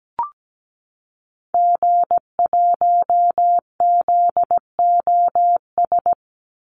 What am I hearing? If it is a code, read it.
Morse code, heard as a keyed tone, about 17 wpm, 700 Hz. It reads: G1ZOS
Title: Call Signs - 17wpm